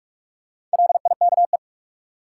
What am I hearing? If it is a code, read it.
Morse code, heard as a keyed tone, about 47 wpm, 700 Hz. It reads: LIKE
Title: Sentences from Top 100 Words - New Word Drill - 45wpm